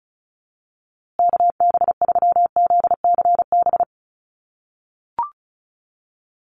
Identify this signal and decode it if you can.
Morse code, heard as a keyed tone, about 34 wpm, 700 Hz. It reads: KB3ZCB
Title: USA Call Signs Repeated 2x 35wpm